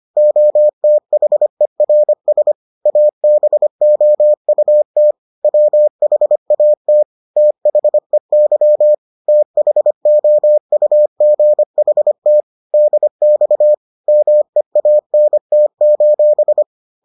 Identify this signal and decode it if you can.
Morse code, heard as a keyed tone, about 25 wpm, 600 Hz. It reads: OTHERS ABOUT WHAT THEY THOUGHT DX MEANT: